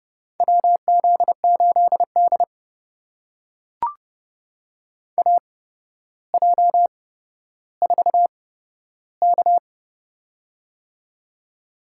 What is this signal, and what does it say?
Morse code, heard as a keyed tone, about 30 wpm, spaced as for 13 wpm, 700 Hz. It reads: WZ8D AJ4K
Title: USA Call Signs - 8x Character Spacing - 30wpm